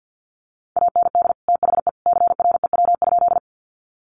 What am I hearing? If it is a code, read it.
Morse code, heard as a keyed tone, about 50 wpm, 700 Hz. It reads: AND THE CREW?
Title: Sentences from Top 1500 Words 50wpm